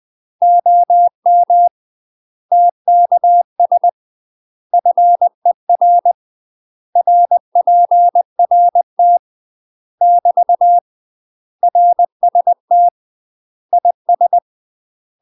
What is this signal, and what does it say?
Morse code, heard as a keyed tone, about 20 wpm, 700 Hz. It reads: OM TKS FER RPRT <BT> RST IS
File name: UK MASTER QSOs TRAINING / 20WPM / UK MASTER QSOs 0064